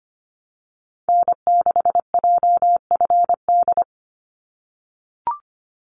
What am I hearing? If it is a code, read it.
Morse code, heard as a keyed tone, about 25 wpm, 700 Hz. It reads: N6JFD
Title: NA SOTA Activator Callsigns 25wpm